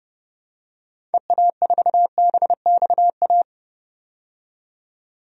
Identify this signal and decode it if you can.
Morse code, heard as a keyed tone, about 30 wpm, 700 Hz. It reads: EA4BXA
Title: DXCC-100 Call Signs - 6 Characters - 30wpm